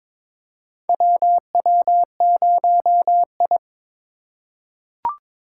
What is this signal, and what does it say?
Morse code, heard as a keyed tone, about 22 wpm, 700 Hz. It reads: WW0I
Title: USA Call Signs - 4 Characters - 4x Character Spacing - 22wpm